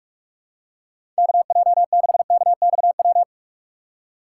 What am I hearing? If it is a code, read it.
Morse code, heard as a keyed tone, about 47 wpm, 700 Hz. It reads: KJ6KXW